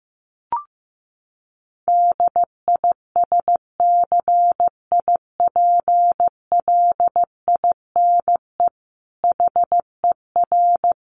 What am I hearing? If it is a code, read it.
Morse code, heard as a keyed tone, about 15 wpm, 700 Hz. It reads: DISCIPLINE HER
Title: Sentences from Top 900 Words 15wpm